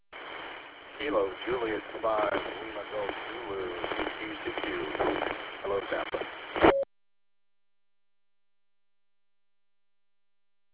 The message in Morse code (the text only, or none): none